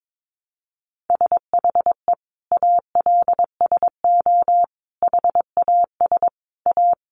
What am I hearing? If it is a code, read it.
Morse code, heard as a keyed tone, about 22 wpm, 700 Hz. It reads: SHE ALSO HAS A
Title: Easy Phrases 22wpm